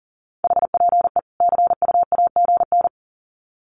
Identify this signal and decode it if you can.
Morse code, heard as a keyed tone, about 37 wpm, 700 Hz. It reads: HPE CU AGN
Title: Common QSO Phrases 40wpm